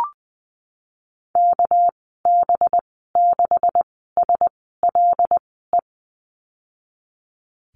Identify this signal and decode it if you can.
Morse code, heard as a keyed tone, about 19 wpm, spaced as for 14 wpm, 700 Hz. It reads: KB6SLE
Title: 200 Lesson 32g - Letter k - Call Signs 20wpm